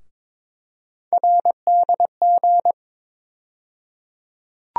Morse code, heard as a keyed tone, about 22 wpm, 700 Hz. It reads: RDG
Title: USA Call Sign Postfixes - 3 Letters 22wpm